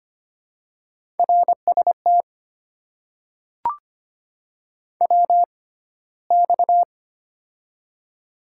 Morse code, heard as a keyed tone, about 25 wpm, spaced as for 15 wpm, 700 Hz. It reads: RST WX
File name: Top 90 QSO Elements - 6x Character Spacing - 25wpm